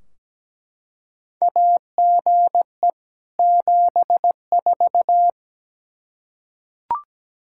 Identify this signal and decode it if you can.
Morse code, heard as a keyed tone, about 17 wpm, 700 Hz. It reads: AGE 74
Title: Practice QSO - Exchange 3 - More Info - 17wpm